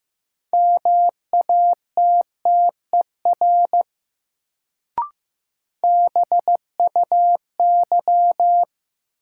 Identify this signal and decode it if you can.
Morse code, heard as a keyed tone, about 15 wpm, 700 Hz. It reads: MATTER BUY
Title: Sentences from Top 700 Words - New Word Drill - 15wpm